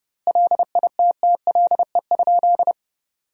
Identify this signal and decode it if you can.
Morse code, heard as a keyed tone, about 32 wpm, 700 Hz. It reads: LITTLE?
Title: 3x Word-Spacing - Sentences from Top 500 Words - 30wpm